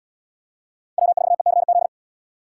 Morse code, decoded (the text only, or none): W4ECZ